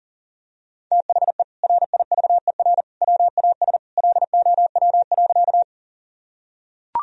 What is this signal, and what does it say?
Morse code, heard as a keyed tone, about 40 wpm, 700 Hz. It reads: THE RIVER WAS LOW.
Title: Sentences from Top 1000 Words - Review Part 1 - 40wpm